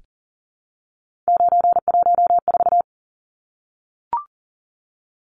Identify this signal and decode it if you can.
Morse code, heard as a keyed tone, about 40 wpm, 700 Hz. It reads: 914
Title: Numbers - 3-Digit 40wpm